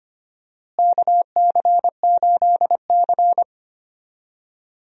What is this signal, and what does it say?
Morse code, heard as a keyed tone, about 25 wpm, 700 Hz. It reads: KC8C
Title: USA Call Signs - 4 Characters - Repeated 3x - 25wpm